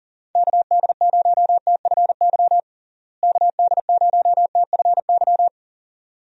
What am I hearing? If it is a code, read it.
Morse code, heard as a keyed tone, about 40 wpm, 700 Hz. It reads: KD0TFY KD0TFY
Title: USA Call Signs - 6 Characters - Repeated 2x - 40wpm